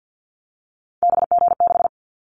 Morse code, heard as a keyed tone, about 50 wpm, 700 Hz. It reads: BG6
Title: Callsign Prefixes for Top 100 DXCC - 50wpm